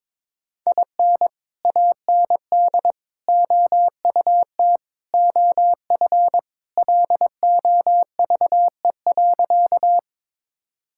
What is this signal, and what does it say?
Morse code, heard as a keyed tone, about 22 wpm, 700 Hz. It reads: IN AND OUT OF LOVE.
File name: practice / 22wpm / Sentences with 4-Word Phrases - 22wpm